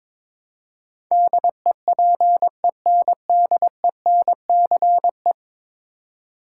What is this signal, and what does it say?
Morse code, heard as a keyed tone, about 22 wpm, 700 Hz. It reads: DEPENDENCE
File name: practice / 22wpm / 10 Letter Words - First 3 Letters Given - 22wpm